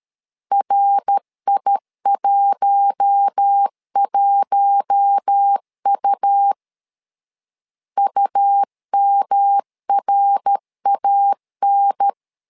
Morse code, heard as a keyed tone, about 13 wpm, 800 Hz. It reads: RI11U UMRAN
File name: LEZIONI / 4 Q1 12wpm 2028